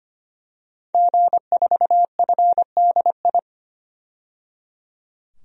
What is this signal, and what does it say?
Morse code, heard as a keyed tone, about 25 wpm, 700 Hz. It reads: G4FDI
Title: DXCC-100 Call Signs - 5 Characters - 25wpm